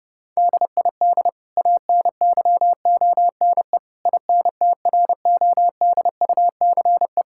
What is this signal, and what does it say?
Morse code, heard as a keyed tone, about 30 wpm, 700 Hz. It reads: DID ANYONE INTRODUCE
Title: Sentences from Top 1600 Words 30wpm